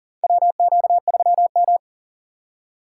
Morse code, heard as a keyed tone, about 40 wpm, 700 Hz. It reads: WQ3M
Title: USA Call Signs - 4 Characters - 6x Character Spacing - 40wpm